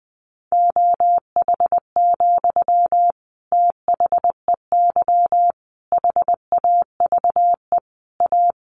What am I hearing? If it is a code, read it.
Morse code, heard as a keyed tone, about 20 wpm, 700 Hz. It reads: OH, THEY HAVE A
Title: Sentences from Top 800 Words 20wpm